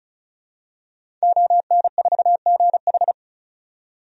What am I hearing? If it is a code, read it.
Morse code, heard as a keyed tone, about 34 wpm, 700 Hz. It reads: ON4GH
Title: DXCC-100 Call Signs - 5 Characters - 35wpm